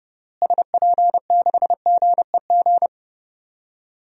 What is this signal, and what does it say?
Morse code, heard as a keyed tone, about 31 wpm, 700 Hz. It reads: SP6GEG
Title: DXCC-100 Call Signs - 6 Characters - Repeated 3x - 30wpm